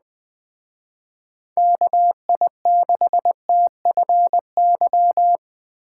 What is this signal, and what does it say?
Morse code, heard as a keyed tone, about 21 wpm, 700 Hz. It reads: KI6TFY
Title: Speed Racing - USA 2x3 Call Signs 17wpm